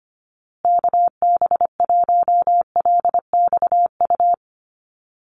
Morse code, heard as a keyed tone, about 25 wpm, 700 Hz. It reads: KB1LXU
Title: USA Call Signs Repeated 3x 25wpm